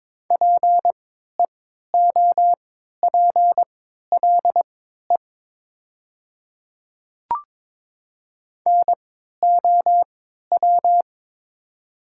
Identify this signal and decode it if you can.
Morse code, heard as a keyed tone, about 22 wpm, spaced as for 12 wpm, 700 Hz. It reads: PEOPLE NOW
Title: 3x Character Spacing - Top 100 Words - 22wpm